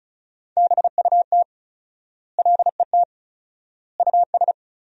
Morse code, heard as a keyed tone, about 34 wpm, 700 Hz. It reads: BUT LET US
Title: 4x Word-Spacing - Sentences from Top 300 Words - 35wpm